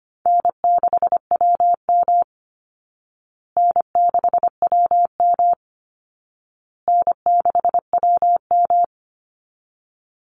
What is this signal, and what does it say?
Morse code, heard as a keyed tone, about 25 wpm, 700 Hz. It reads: N6WM N6WM N6WM
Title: Easy Call Signs Repeated 3x - 25wpm